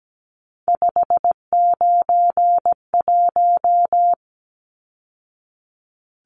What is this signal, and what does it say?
Morse code, heard as a keyed tone, about 17 wpm, 700 Hz. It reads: 591
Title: Numbers - 3-Digit 17wpm